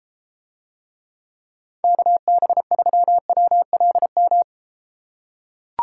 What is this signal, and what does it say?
Morse code, heard as a keyed tone, about 32 wpm, 700 Hz. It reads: KB3WLM